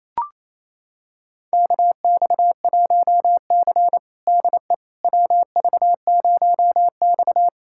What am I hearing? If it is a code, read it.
Morse code, heard as a keyed tone, about 28 wpm, 700 Hz. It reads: KX1C DE WV0X
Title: USA Callsign DE Callsign - 4 Characters - 28wpm